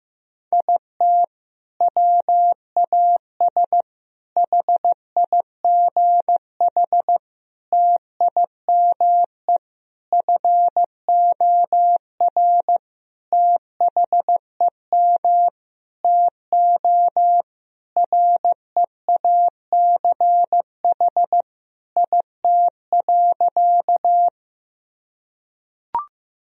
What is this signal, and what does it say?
Morse code, heard as a keyed tone, about 15 wpm, 700 Hz. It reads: IT WAS HIGH TIME FOR THEM TO REACH IT.
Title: Sentences from Top 1000 Words - Review Part 2 - 15wpm